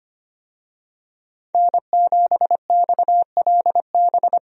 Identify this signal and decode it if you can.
Morse code, heard as a keyed tone, about 25 wpm, 700 Hz. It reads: N7XLB
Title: USA Call Signs - 5 Characters - 25wpm